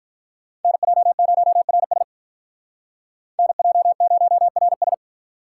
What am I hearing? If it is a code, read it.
Morse code, heard as a keyed tone, about 43 wpm, 700 Hz. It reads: NJ0RS NJ0RS
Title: Speed Racing - USA 2x2 Call Signs 35wpm